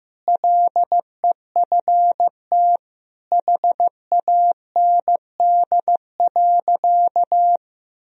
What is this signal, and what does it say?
Morse code, heard as a keyed tone, about 15 wpm, 700 Hz. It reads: LEFT HAND.
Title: Sentences from Top 1000 Words - Review Part 2 - 15wpm